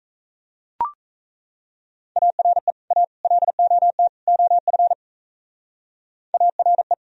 Morse code, heard as a keyed tone, about 35 wpm, 700 Hz. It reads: ARE A LOT OF ARE
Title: Speed-Racing - Sets of 4 Words 28wpm